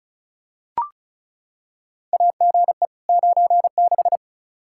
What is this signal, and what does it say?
Morse code, heard as a keyed tone, about 36 wpm, 700 Hz. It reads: AGE 96
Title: Practice QSO - Exchange 3 - More Info - 35wpm